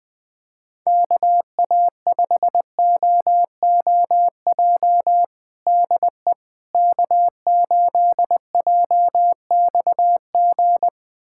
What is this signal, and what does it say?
Morse code, heard as a keyed tone, about 20 wpm, 700 Hz. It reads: KA5OOJ DE K8JXG
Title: USA Callsign DE Callsign 20wpm